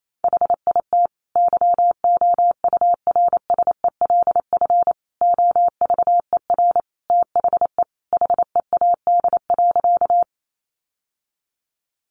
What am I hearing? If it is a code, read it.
Morse code, heard as a keyed tone, about 28 wpm, 700 Hz. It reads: HIT YOURSELF OVER THE HEAD.
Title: Sentences from Top 1200 Words 28wpm